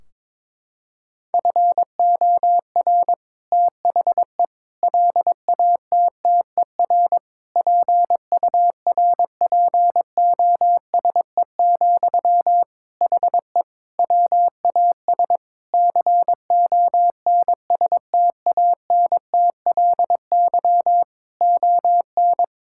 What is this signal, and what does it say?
Morse code, heard as a keyed tone, about 22 wpm, 700 Hz. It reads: FOR THE LATTER PURPOSE, HE WAS CONSTANTLY ON
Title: Sentences from Top 1600 Words 22wpm